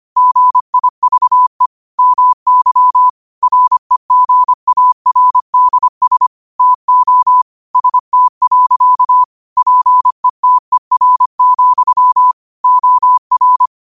Morse code, heard as a keyed tone, about 25 wpm, 1000 Hz. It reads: GIVE MY REGARDS TO ST. PETER, OR
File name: qotdfiles / 25wpm